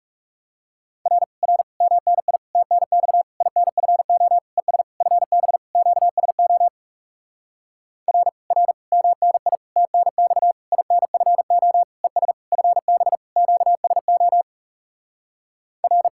Morse code, heard as a keyed tone, about 40 wpm, 700 Hz. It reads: R R MNI TNX INFO ES FB QSO R R MNI TNX INFO ES FB QSO R